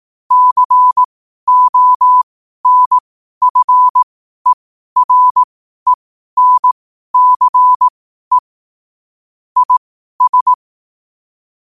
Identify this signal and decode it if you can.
Morse code, heard as a keyed tone, about 18 wpm, spaced as for 12 wpm, 1000 Hz. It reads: CONFERENCE IS
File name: qotdfiles / 12wpm